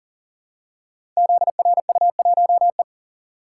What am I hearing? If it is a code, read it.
Morse code, heard as a keyed tone, about 42 wpm, 700 Hz. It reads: ZRU1E
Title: Code Groups - 5 Letter:Numbers - 40wpm